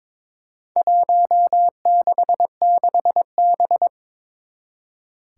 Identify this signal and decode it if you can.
Morse code, heard as a keyed tone, about 21 wpm, 700 Hz. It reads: 166B
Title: Four Challenging Letter-Number Combinations 22wpm